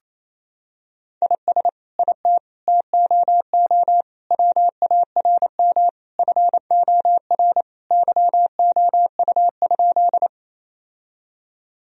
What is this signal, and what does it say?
Morse code, heard as a keyed tone, about 28 wpm, 700 Hz. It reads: IS IT TOO WARM FOR YOU?
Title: Sentences from Top 1000 Words - Review Part 2 - 28wpm